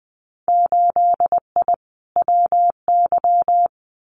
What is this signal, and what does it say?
Morse code, heard as a keyed tone, about 20 wpm, 700 Hz. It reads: 8I WY
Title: Winter Field Day Exchanges Encore 20wpm